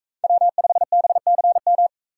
Morse code, heard as a keyed tone, about 38 wpm, 700 Hz. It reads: W5BCM